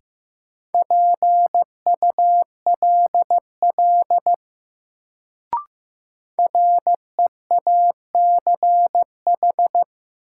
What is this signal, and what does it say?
Morse code, heard as a keyed tone, about 15 wpm, 700 Hz. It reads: PULL REACH
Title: Sentences from Top 400 Words - New Word Drill - 15wpm